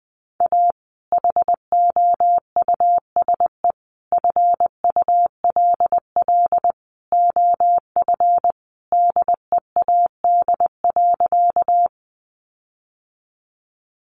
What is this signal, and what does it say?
Morse code, heard as a keyed tone, about 20 wpm, 700 Hz. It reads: A HOUSE FULL OF DEAD.
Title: Sentences from Top 1500 Words 20wpm